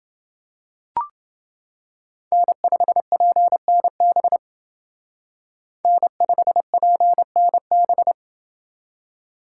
Morse code, heard as a keyed tone, about 29 wpm, 700 Hz. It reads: N5PNB N5PNB